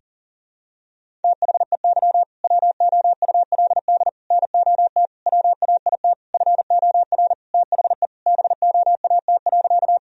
Morse code, heard as a keyed tone, about 40 wpm, 700 Hz. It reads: THEY WOULD NOT WAIT FOR THE BOAT.
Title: Sentences from Top 1000 Words - Review Part 3 - 40wpm